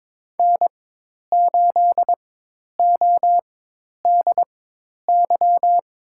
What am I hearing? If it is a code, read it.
Morse code, heard as a keyed tone, about 22 wpm, spaced as for 10 wpm, 700 Hz. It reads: N8ODY
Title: USA Call Signs - 5 Characters - 4x Character Spacing - 22wpm